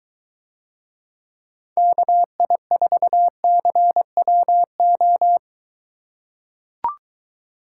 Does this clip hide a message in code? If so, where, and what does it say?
Yes: Morse code from 1.8 to 5.4 s, heard as a keyed tone, about 23 wpm, 700 Hz. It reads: KI4CWO